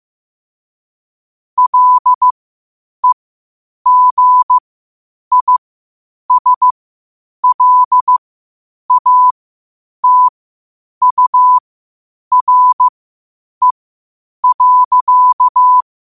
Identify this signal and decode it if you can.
Morse code, heard as a keyed tone, about 15 wpm, spaced as for 8 wpm, 1000 Hz. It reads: LEGISLATURE.
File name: qotdfiles / 8wpm